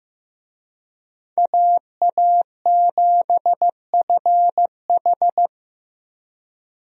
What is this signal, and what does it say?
Morse code, heard as a keyed tone, about 15 wpm, 700 Hz. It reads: AA7FH
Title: USA Call Signs - 5 Characters - Repeated 3x - 15wpm